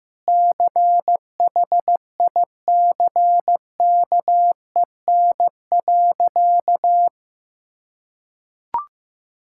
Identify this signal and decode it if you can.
Morse code, heard as a keyed tone, about 15 wpm, 700 Hz. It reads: CHICKEN.
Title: Sentences from Top 1000 Words - Review Part 5 - 15wpm